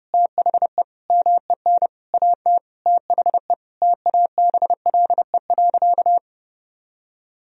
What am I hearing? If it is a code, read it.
Morse code, heard as a keyed tone, about 29 wpm, 700 Hz. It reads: THE MEN AT THE TABLE.
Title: Sentences from Top 1000 Words - Review Part 2 - 30wpm